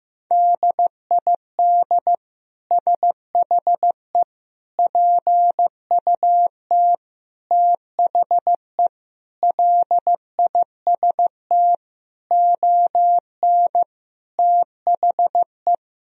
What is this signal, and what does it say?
Morse code, heard as a keyed tone, about 15 wpm, 700 Hz. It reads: DID SHE PUT THE LIST ON THE
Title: Sentences from Top 1000 Words - Review Part 2 - 15wpm